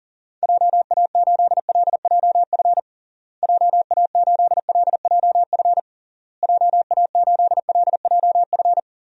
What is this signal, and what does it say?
Morse code, heard as a keyed tone, about 40 wpm, 700 Hz. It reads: JA8LJF JA8LJF JA8LJF
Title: DXCC-100 Call Signs - 6 Characters - Repeated 3x - 40wpm